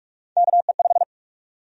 Morse code, heard as a keyed tone, about 46 wpm, 700 Hz. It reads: KE5
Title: USA Call Sign Prefixes - Repeated 2x - 45wpm